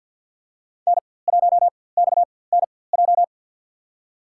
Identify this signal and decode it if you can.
Morse code, heard as a keyed tone, about 53 wpm, spaced as for 24 wpm, 690 Hz. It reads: N1XNJ
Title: USA Call Signs - 4x Character Spacing - 50wpm